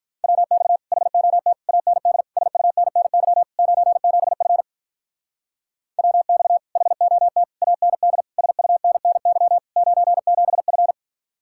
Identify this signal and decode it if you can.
Morse code, heard as a keyed tone, about 43 wpm, 700 Hz. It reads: WX HOT AND SUNNY 97F WX HOT AND SUNNY 97F
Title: Speed-Racing Common QSO Phrases - 35wpm